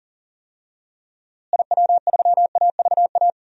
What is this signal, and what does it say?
Morse code, heard as a keyed tone, about 40 wpm, 700 Hz. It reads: IW3AVA
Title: DXCC-100 Call Signs - 6 Characters - Repeated 2x - 40wpm